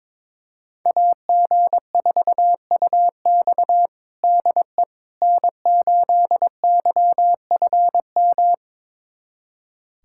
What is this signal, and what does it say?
Morse code, heard as a keyed tone, about 22 wpm, 700 Hz. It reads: AG4UX DE N8YFM
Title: USA Callsign DE Callsign - 5 Characters - 22wpm